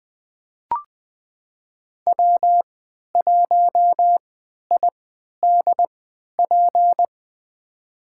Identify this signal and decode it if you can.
Morse code, heard as a keyed tone, about 20 wpm, spaced as for 11 wpm, 700 Hz. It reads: W1IDP
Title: USA Call Signs - 5 Characters - 3x Character Spacing - 20wpm